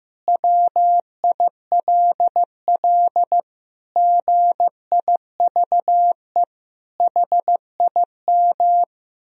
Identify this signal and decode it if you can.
Morse code, heard as a keyed tone, about 16 wpm, 700 Hz. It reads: WILL GIVE HIM
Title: Sentences from Top 1000 Words - Review Part 4 - 15wpm